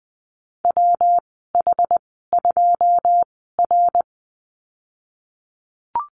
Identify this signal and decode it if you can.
Morse code, heard as a keyed tone, about 20 wpm, spaced as for 15 wpm, 700 Hz. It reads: WH2R
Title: 200 Lesson 32g - Letter k - Call Signs 20wpm